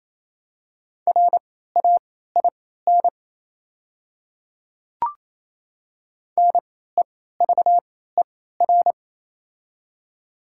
Morse code, heard as a keyed tone, about 28 wpm, spaced as for 16 wpm, 700 Hz. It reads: RAIN NEVER
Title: 3x Character Spacing - Top 300 Words - 28wpm